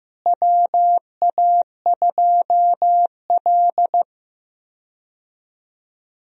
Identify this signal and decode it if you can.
Morse code, heard as a keyed tone, about 15 wpm, 700 Hz. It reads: WA2L
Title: USA Call Signs - 4 Characters - Repeated 2x - 15wpm